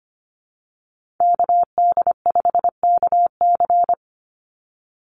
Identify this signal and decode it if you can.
Morse code, heard as a keyed tone, about 25 wpm, 700 Hz. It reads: KD5KC
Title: NA SOTA Activator Callsigns 25wpm